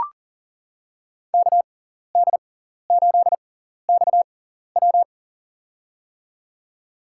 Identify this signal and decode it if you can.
Morse code, heard as a keyed tone, about 40 wpm, spaced as for 14 wpm, 700 Hz. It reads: KD8XW
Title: USA Call Signs - 5 Characters - 6x Character Spacing - 40wpm